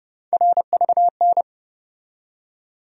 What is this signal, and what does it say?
Morse code, heard as a keyed tone, about 31 wpm, 700 Hz. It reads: RVN
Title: USA Call Sign Postfixes - 3 Letters 30wpm